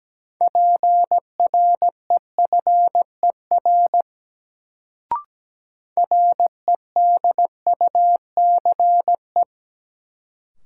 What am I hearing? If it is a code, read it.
Morse code, heard as a keyed tone, about 17 wpm, 700 Hz. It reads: PREFER REDUCE
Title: Sentences from Top 1100 Words - New Word Drill - 17wpm